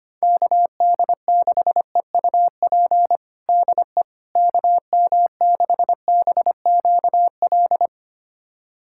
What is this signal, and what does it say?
Morse code, heard as a keyed tone, about 25 wpm, 700 Hz. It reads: KD6EUP DE KM6BQL
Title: USA Callsign DE Callsign - 6 Characters - 25wpm